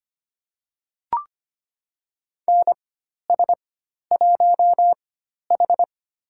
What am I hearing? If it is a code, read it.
Morse code, heard as a keyed tone, about 25 wpm, spaced as for 12 wpm, 700 Hz. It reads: NS1H
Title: USA Call Signs - 4 Characters - 4x Character Spacing - 25wpm